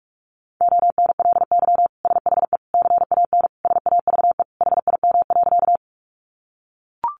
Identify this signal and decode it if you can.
Morse code, heard as a keyed tone, about 44 wpm, 700 Hz. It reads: ONLY SHE CAN SAVE HIM.
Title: Sentences from Top 1100 Words 45wpm